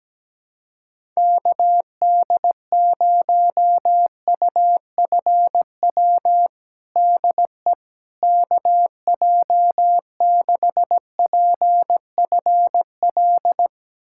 Morse code, heard as a keyed tone, about 17 wpm, 700 Hz. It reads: KD0UFW DE KJ6PFL
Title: USA Callsign DE Callsign - 6 Characters - 17wpm